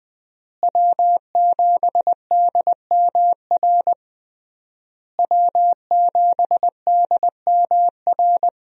Morse code, heard as a keyed tone, about 20 wpm, 700 Hz. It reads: W7DMR W7DMR
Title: USA Call Signs - 5 Characters - Repeated 3x - 20wpm